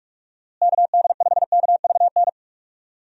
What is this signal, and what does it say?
Morse code, heard as a keyed tone, about 46 wpm, 700 Hz. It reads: KD5KVN
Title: USA Call Signs - 6 Characters - Repeated 3x - 45wpm